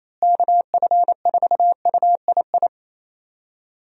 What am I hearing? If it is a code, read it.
Morse code, heard as a keyed tone, about 28 wpm, 700 Hz. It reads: KF4UII
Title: USA Call Signs - 6 Characters - Repeated 2x - 28wpm